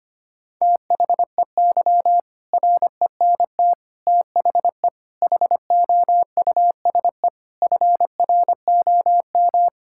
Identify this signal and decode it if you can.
Morse code, heard as a keyed tone, about 25 wpm, 700 Hz. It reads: THEY RENT THE HOUSE FROM
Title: Sentences from Top 1000 Words - Review Part 5 - 25wpm